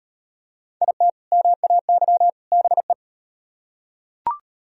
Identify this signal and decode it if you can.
Morse code, heard as a keyed tone, about 38 wpm, 700 Hz. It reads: IT MAY BE